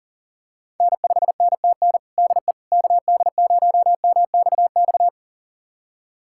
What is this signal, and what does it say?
Morse code, heard as a keyed tone, about 40 wpm, 700 Hz. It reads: N5NTN DE KD0MXX
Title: USA Callsign DE Callsign 40wpm